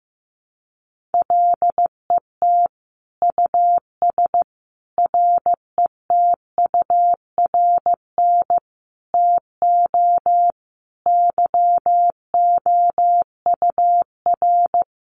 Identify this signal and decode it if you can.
Morse code, heard as a keyed tone, about 15 wpm, 700 Hz. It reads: LET US RETURN TO YOUR
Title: Sentences from Top 800 Words 15wpm